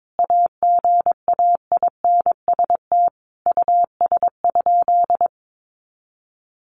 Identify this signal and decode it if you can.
Morse code, heard as a keyed tone, about 22 wpm, 700 Hz. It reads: AGAINST US?
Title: Sentences from Top 1500 Words 22wpm